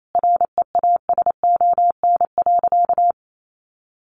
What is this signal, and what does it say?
Morse code, heard as a keyed tone, about 28 wpm, 700 Hz. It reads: REASON.
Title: Sentences from Top 1300 Words 28wpm